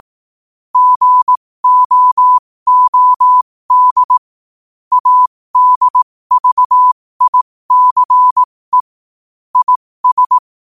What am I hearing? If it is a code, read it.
Morse code, heard as a keyed tone, about 18 wpm, spaced as for 15 wpm, 1000 Hz. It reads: GOOD ADVICE IS
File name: qotdfiles / 15wpm